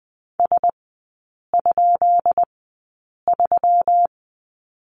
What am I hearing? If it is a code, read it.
Morse code, heard as a keyed tone, about 20 wpm, spaced as for 8 wpm, 700 Hz. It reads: S?3
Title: 184 Lesson 30e - Letter b - Three Characters 20wpm